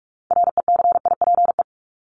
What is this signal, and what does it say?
Morse code, heard as a keyed tone, about 44 wpm, 700 Hz. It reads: RECIPE
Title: Top 1001-1250 Words 45wpm